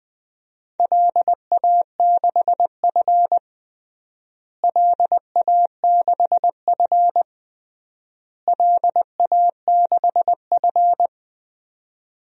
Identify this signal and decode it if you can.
Morse code, heard as a keyed tone, about 21 wpm, 700 Hz. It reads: LA6F LA6F LA6F
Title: DXCC-100 Call Signs - 4 Characters - Repeated 3x - 20wpm